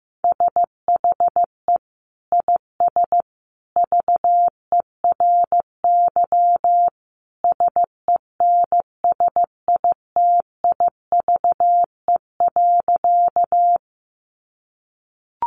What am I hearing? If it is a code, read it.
Morse code, heard as a keyed tone, about 15 wpm, 700 Hz. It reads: SHE IS VERY SENSITIVE.
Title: Sentences from Top 1400 Words 15wpm